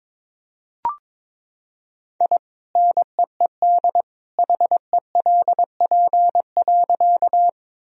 Morse code, heard as a keyed tone, about 22 wpm, 700 Hz. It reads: I NEED HELP.
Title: Sentences from Top 1000 Words - Review Part 1 - 22wpm